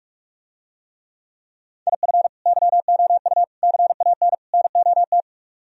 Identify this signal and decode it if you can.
Morse code, heard as a keyed tone, about 44 wpm, 700 Hz. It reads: IF YOU CAN NOT